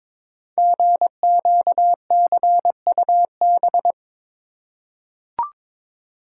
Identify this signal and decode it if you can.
Morse code, heard as a keyed tone, about 22 wpm, 700 Hz. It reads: GQCUB
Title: Code Groups - 5 Letter:Numbers - 22wpm